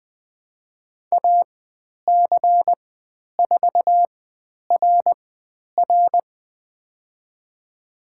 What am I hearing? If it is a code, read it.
Morse code, heard as a keyed tone, about 20 wpm, spaced as for 10 wpm, 700 Hz. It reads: AC4RR
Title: 095 Lesson 16g - Letter c - Call Signs 20wpm